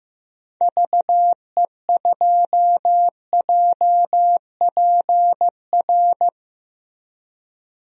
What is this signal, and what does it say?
Morse code, heard as a keyed tone, about 15 wpm, 700 Hz. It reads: VE2JPR
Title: DXCC-100 Call Signs - 6 Characters - 15wpm